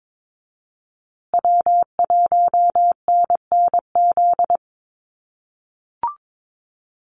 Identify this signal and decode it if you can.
Morse code, heard as a keyed tone, about 22 wpm, 700 Hz. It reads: W1NNZ
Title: USA Call Signs Repeated 3x 22wpm